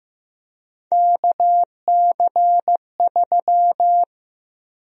Morse code, heard as a keyed tone, about 15 wpm, 700 Hz. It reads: KC3
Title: USA Call Sign Prefixes - Repeated 2x - 15wpm